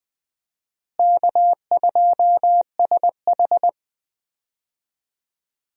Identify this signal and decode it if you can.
Morse code, heard as a keyed tone, about 21 wpm, 700 Hz. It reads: K2SH
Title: USA Call Signs - 4 Characters - Repeated 3x - 20wpm